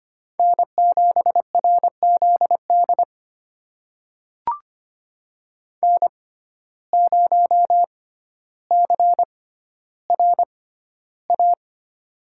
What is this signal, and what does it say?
Morse code, heard as a keyed tone, about 25 wpm, spaced as for 13 wpm, 700 Hz. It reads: N7RZD N0CRA
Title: USA Call Signs - 5 Characters - 6x Character Spacing - 25wpm